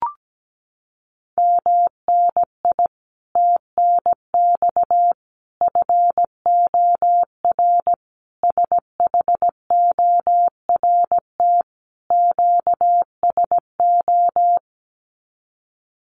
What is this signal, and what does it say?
Morse code, heard as a keyed tone, about 17 wpm, 700 Hz. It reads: MNI TNX FOR SHORT QSO
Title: Common QSO Phrases 17wpm